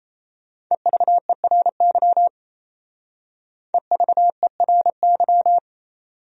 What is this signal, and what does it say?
Morse code, heard as a keyed tone, about 36 wpm, 700 Hz. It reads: EVERY EVERY